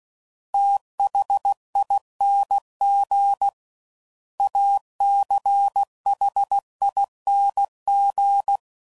Morse code, heard as a keyed tone, about 16 wpm, 800 Hz. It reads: THING ACHING
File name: Suffixes / Special INGs / Sngl Ings